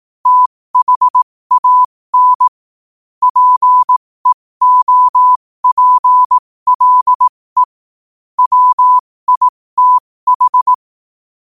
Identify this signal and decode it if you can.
Morse code, heard as a keyed tone, about 18 wpm, spaced as for 15 wpm, 1000 Hz. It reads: THAN PEOPLE WITH
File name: qotdfiles / 15wpm